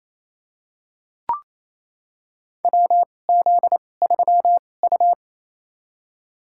Morse code, heard as a keyed tone, about 28 wpm, spaced as for 20 wpm, 700 Hz. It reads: WZ3U
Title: USA Call Signs - 2x Character Spacing - 28wpm